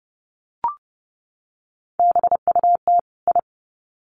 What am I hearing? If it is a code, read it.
Morse code, heard as a keyed tone, about 29 wpm, 700 Hz. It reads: BUT I
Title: Sets of 2 Words 30wpm